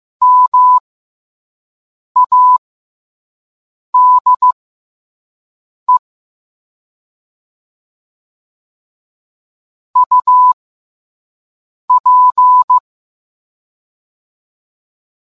Morse code, heard as a keyed tone, about 15 wpm, spaced as for 5 wpm, 1000 Hz. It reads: MADE UP